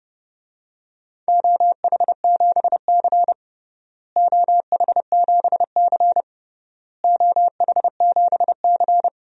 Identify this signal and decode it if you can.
Morse code, heard as a keyed tone, about 30 wpm, 700 Hz. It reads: OH7C OH7C OH7C
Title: DXCC-100 Call Signs - 4 Characters - Repeated 3x - 30wpm